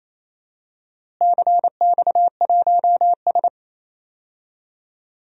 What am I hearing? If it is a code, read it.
Morse code, heard as a keyed tone, about 28 wpm, 700 Hz. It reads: CX1S
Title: DXCC-100 Call Signs - 4 Characters - Repeated 3x - 28wpm